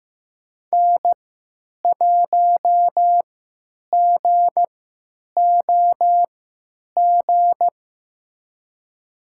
Morse code, heard as a keyed tone, about 15 wpm, spaced as for 9 wpm, 700 Hz. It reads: N1GOG
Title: USA Call Signs - 5 Characters - 3x Character Spacing - 15wpm